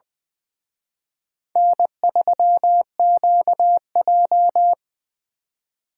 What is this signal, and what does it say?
Morse code, heard as a keyed tone, about 23 wpm, 700 Hz. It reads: N3QJ
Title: Speed Racing - USA 1x2 Call Signs 20wpm